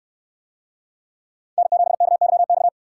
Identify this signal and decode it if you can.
Morse code, heard as a keyed tone, about 68 wpm, 700 Hz. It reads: N7KYX